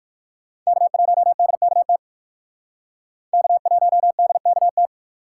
Morse code, heard as a keyed tone, about 60 wpm, 700 Hz. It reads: K1DKT K1DKT